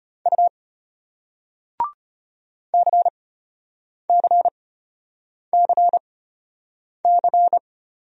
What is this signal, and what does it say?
Morse code, heard as a keyed tone, about 33 wpm, spaced as for 8 wpm, 700 Hz. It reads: U CCCC